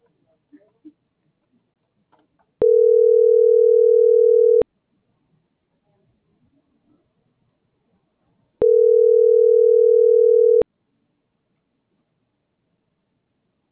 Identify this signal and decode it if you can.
Ringback tone (North America), 440 + 480 Hz.